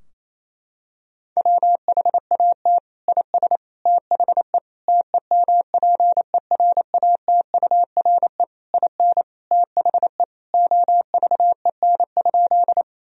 Morse code, heard as a keyed tone, about 28 wpm, 700 Hz. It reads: WHAT IS THE TEMPERATURE IN THE OVEN?
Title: Sentences from Top 1000 Words - Review Part 4 - 28wpm